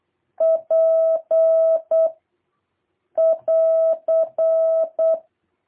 Morse code, heard as a keyed tone, about 8 wpm, 650 Hz. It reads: P <AR>
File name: jh8cbh / 2023cw